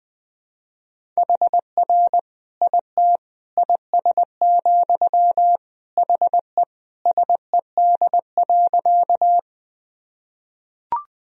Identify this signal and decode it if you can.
Morse code, heard as a keyed tone, about 21 wpm, 700 Hz. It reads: HR IT IS, HE SED.
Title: Sentences from Top 200 Words and Abbreviations 20wpm